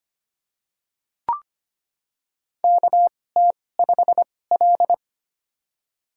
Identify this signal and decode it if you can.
Morse code, heard as a keyed tone, about 25 wpm, spaced as for 18 wpm, 700 Hz. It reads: KT5L
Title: USA Call Signs - 4 Characters - 2x Character Spacing - 25wpm